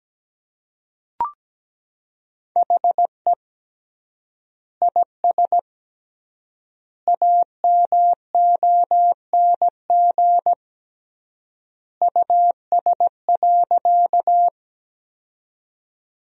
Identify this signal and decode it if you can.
Morse code, heard as a keyed tone, about 17 wpm, 700 Hz. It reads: HE IS AMONG US.
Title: 3x Word-Spacing - Sentences from Top 500 Words - 17wpm